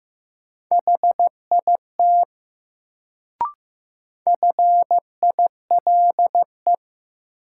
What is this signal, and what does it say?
Morse code, heard as a keyed tone, about 15 wpm, 700 Hz. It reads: HIT FILE
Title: Sentences from Top 1200 Words - New Word Drill Alt - 15wpm